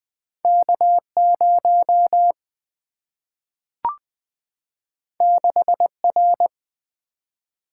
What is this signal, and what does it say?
Morse code, heard as a keyed tone, about 20 wpm, 700 Hz. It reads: K 0 6 R
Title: Single-Letter Single-Number 20wpm